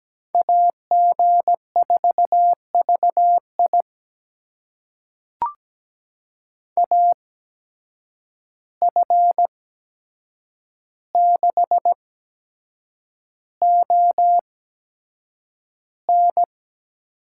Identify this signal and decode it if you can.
Morse code, heard as a keyed tone, about 17 wpm, spaced as for 7 wpm, 700 Hz. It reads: AG4VI AF6ON
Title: USA Call Signs - 5 Characters - 8x Character Spacing - 17wpm